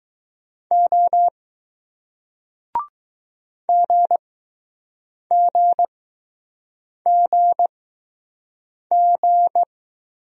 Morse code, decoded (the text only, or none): O GGGG